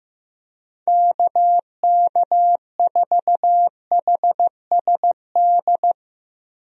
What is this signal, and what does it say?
Morse code, heard as a keyed tone, about 15 wpm, 700 Hz. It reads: KK4HSD
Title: USA Call Signs - 6 Characters - 15wpm